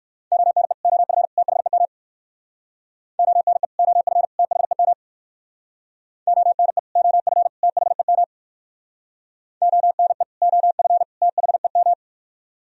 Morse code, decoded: ONE OF THEM ONE OF THEM ONE OF THEM ONE OF THEM

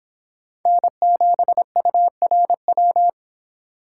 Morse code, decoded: N7URW

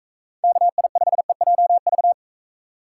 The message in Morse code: KI5EJV

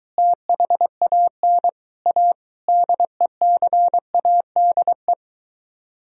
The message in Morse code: THAN A DECADE

